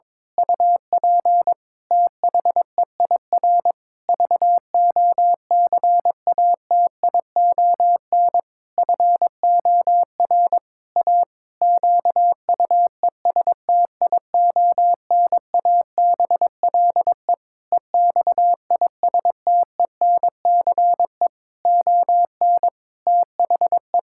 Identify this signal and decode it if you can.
Morse code, heard as a keyed tone, about 22 wpm, 700 Hz. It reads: UP THEIR VOCATION FOR A QUESTIONABLE EXISTENCE ON THE
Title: Pangram Challenge v2 - 22wpm